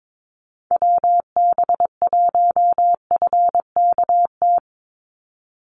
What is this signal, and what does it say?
Morse code, heard as a keyed tone, about 22 wpm, 700 Hz. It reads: WB1FKT
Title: USA Call Signs 22wpm